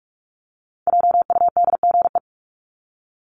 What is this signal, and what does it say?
Morse code, heard as a keyed tone, about 44 wpm, 700 Hz. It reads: JUDGE
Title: Top 1751-2000 Words 45wpm